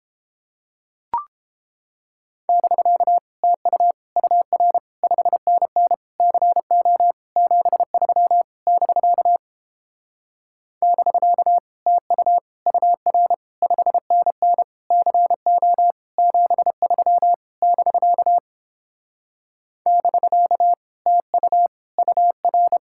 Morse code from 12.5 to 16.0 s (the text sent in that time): UR 5NN CO